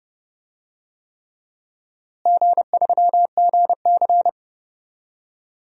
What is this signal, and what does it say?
Morse code, heard as a keyed tone, about 30 wpm, 700 Hz. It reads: G3GC
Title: DXCC-100 Call Signs - 4 Characters - Repeated 3x - 30wpm